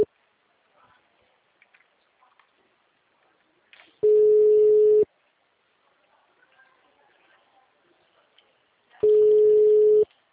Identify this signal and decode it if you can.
Ringback tone (Europe, 425 Hz).
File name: out- -2018- - - .82915